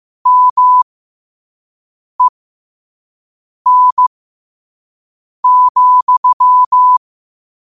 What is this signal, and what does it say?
Morse code, heard as a keyed tone, about 15 wpm, spaced as for 5 wpm, 1000 Hz. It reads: MEN,